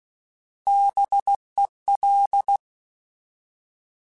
Morse code, heard as a keyed tone, about 16 wpm, 800 Hz. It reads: BEL
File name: Alphabetical / Alphabetical mp3s / Bs @15